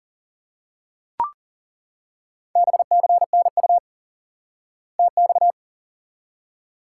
Morse code, heard as a keyed tone, about 40 wpm, 700 Hz. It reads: BCNU TX
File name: IQE Course 3 - Lesson 89 - YAGI - Two Elements - 40wpm